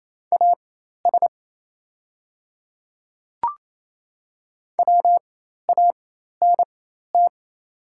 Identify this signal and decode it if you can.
Morse code, heard as a keyed tone, about 28 wpm, spaced as for 13 wpm, 690 Hz. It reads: AS WANT
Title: 4x Character Spacing - Top 300 Words - 28wpm